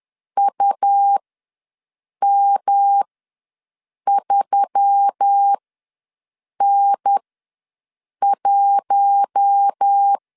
Morse code, heard as a keyed tone, about 10 wpm, spaced as for 6 wpm, 800 Hz. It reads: UM3N1
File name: LEZIONI / 6 PKX0 / 6 P ws 10wpm 3751